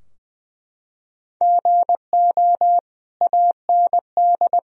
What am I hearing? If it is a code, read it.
Morse code, heard as a keyed tone, about 20 wpm, 700 Hz. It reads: GO AND
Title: Sentences from Top 1000 Words - Review Part 4 - 20wpm